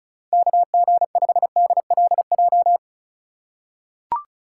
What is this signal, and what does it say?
Morse code, heard as a keyed tone, about 36 wpm, 700 Hz. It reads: KG5DLJ
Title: USA Call Signs - 6 Characters - 4x Character Spacing - 35wpm